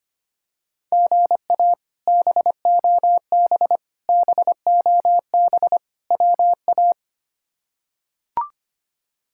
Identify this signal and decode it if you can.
Morse code, heard as a keyed tone, about 25 wpm, 700 Hz. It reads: GA BOB BOB WA
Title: K1USN SST QSO Practice - 25wpm